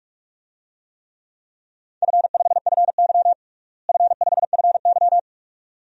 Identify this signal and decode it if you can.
Morse code, heard as a keyed tone, about 46 wpm, 700 Hz. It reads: F5FY F5FY
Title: DXCC-100 Call Signs - 4 Characters - Repeated 2x - 45wpm